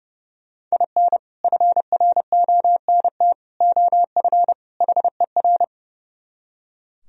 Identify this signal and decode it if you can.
Morse code, heard as a keyed tone, about 40 wpm, spaced as for 35 wpm, 700 Hz. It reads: IN FRONT OF HER
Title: Speed-Racing - Sets of 4 Words 30wpm